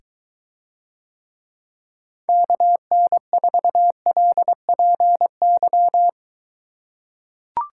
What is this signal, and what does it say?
Morse code, heard as a keyed tone, about 23 wpm, 700 Hz. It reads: KN4LPY